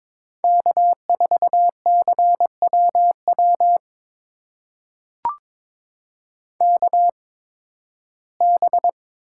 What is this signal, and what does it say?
Morse code, heard as a keyed tone, about 23 wpm, spaced as for 15 wpm, 700 Hz. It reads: K4CWW KB
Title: USA Call Signs - 8x Character Spacing - 22wpm